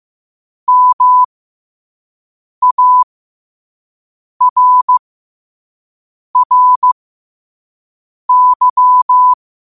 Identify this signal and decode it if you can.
Morse code, heard as a keyed tone, about 15 wpm, spaced as for 5 wpm, 1000 Hz. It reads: MARRY
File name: qotdfiles / 5wpm